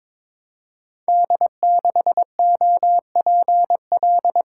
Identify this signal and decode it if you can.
Morse code, heard as a keyed tone, about 23 wpm, 700 Hz. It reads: D6OPL
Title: Code Groups - 5 Letter:Numbers - 22wpm